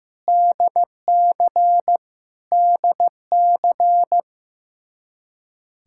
Morse code, heard as a keyed tone, about 15 wpm, 700 Hz. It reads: DC DC
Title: Practice QSO - Exchange 1 - Beginning - 15wpm